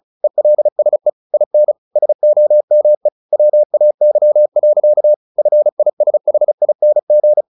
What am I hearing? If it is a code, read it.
Morse code, heard as a keyed tone, about 34 wpm, 600 Hz. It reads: ELSE IN SOME WAY. FISHING
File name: the old man and the sea 2 35wpm